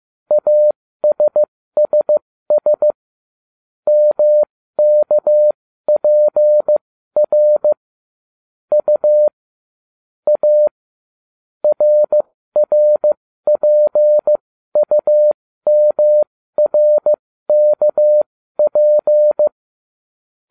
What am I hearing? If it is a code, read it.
Morse code, heard as a keyed tone, about 15 wpm, spaced as for 13 wpm, 600 Hz. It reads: ASSS MKPR U A RRPUMRKP